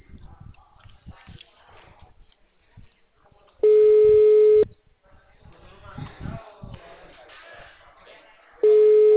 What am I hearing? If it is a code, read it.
Ringback tone (Europe, 425 Hz), with speech.